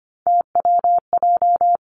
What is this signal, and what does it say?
Morse code, heard as a keyed tone, about 25 wpm, 700 Hz. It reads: TWJ
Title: Three Challenging Letter-Number Combinations 25wpm